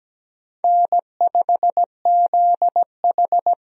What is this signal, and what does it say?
Morse code, heard as a keyed tone, about 17 wpm, 700 Hz. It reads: N5ZH
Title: USA Call Signs - 4 Characters - Repeated 2x - 17wpm